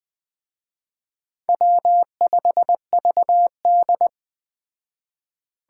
Morse code, heard as a keyed tone, about 20 wpm, 700 Hz. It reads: W5VD